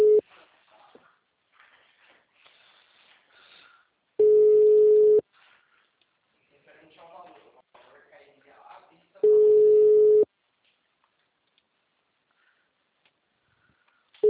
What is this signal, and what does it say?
Ringback tone (Europe, 425 Hz), with speech.
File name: out- -2009- - - .8354